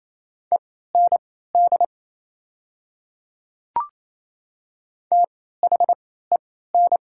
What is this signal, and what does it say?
Morse code, heard as a keyed tone, about 29 wpm, spaced as for 16 wpm, 690 Hz. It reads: END THEN
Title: 3x Character Spacing - Top 200 Words - 28wpm